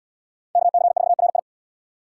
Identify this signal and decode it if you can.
Morse code, heard as a keyed tone, about 66 wpm, 700 Hz. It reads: KC4RI